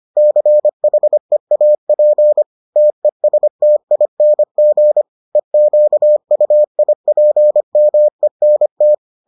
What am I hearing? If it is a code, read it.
Morse code, heard as a keyed tone, about 25 wpm, 600 Hz. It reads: CHEAP TESTING EQUIPMENT